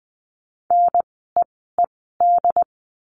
Morse code, heard as a keyed tone, about 20 wpm, spaced as for 15 wpm, 700 Hz. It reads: NEED
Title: 094 Lesson 16f - Letter c - Words 20wpm